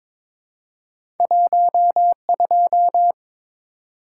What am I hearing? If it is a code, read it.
Morse code, heard as a keyed tone, about 22 wpm, 700 Hz. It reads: 12
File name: practice / 22wpm / Two Letter-Number Combinations - 22wpm